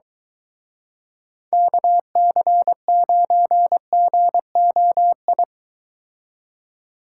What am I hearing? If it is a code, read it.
Morse code, heard as a keyed tone, about 23 wpm, 700 Hz. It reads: KC9GOI